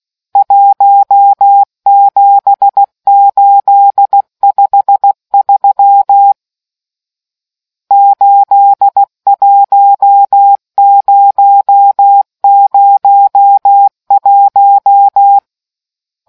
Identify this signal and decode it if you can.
Morse code, heard as a keyed tone, about 16 wpm, 800 Hz. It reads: 17853 81001